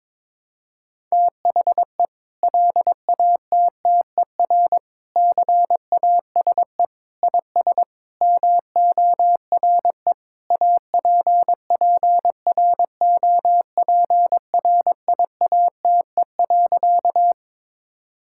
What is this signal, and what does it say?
Morse code, heard as a keyed tone, about 22 wpm, 700 Hz. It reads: THE LATTER CASE IS MORE APPROPRIATE.
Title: Sentences from Top 1600 Words 22wpm